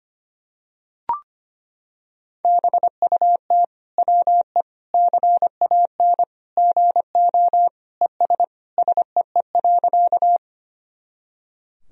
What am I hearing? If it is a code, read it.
Morse code, heard as a keyed tone, about 25 wpm, 700 Hz. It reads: BUT WE CAN GO ES SEE.
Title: Sentences from Top 100 Words and Abbreviations 25wpm